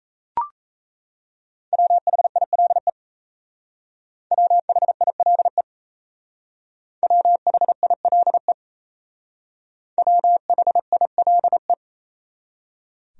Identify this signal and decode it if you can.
Morse code, heard as a keyed tone, about 36 wpm, 700 Hz. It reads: WHILE WHILE WHILE WHILE